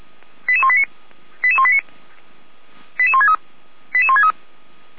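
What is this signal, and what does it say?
Five-tone selective call (ZVEI), heard as quick runs of five tones, tones of about 70 ms. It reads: 88189 88189 88163 88163
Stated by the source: ZVEI five-tone call 88191 received by SDR.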